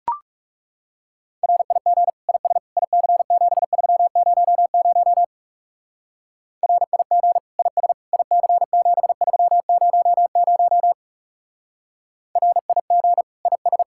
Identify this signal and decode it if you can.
Morse code, heard as a keyed tone, about 36 wpm, 700 Hz. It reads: RIG IS IC7300 RIG IS IC7300 RIG IS
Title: Speed-Racing Common QSO Phrases - 30wpm